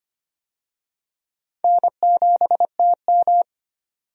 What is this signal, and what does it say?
Morse code, heard as a keyed tone, about 25 wpm, 700 Hz. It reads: N7TM
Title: USA Call Signs - 4 Characters - 25wpm